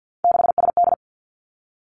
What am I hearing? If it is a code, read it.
Morse code, heard as a keyed tone, about 50 wpm, 700 Hz. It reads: 6SD
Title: Three Challenging Letter-Number Combinations 50wpm